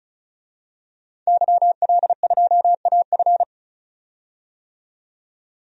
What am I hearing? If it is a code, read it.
Morse code, heard as a keyed tone, about 36 wpm, 700 Hz. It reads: YL2AF
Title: DXCC-100 Call Signs - 5 Characters - 35wpm